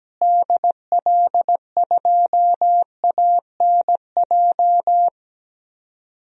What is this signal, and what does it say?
Morse code, heard as a keyed tone, about 17 wpm, 700 Hz. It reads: DL2ANJ
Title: DXCC-100 Call Signs - 6 Characters - Repeated 3x - 17wpm